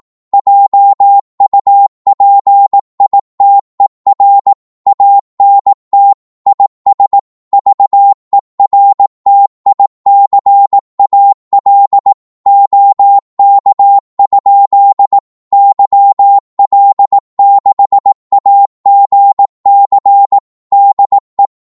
Morse code, heard as a keyed tone, about 18 wpm, 800 Hz. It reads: JUPITER ANT IS VERTICAL OK? YL6AGC DE
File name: Morse / qso-18wpm